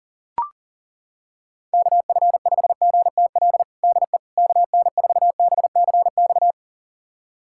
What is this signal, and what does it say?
Morse code, heard as a keyed tone, about 40 wpm, 700 Hz. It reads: KF5GTL DE KN4BCX
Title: USA Callsign DE Callsign 40wpm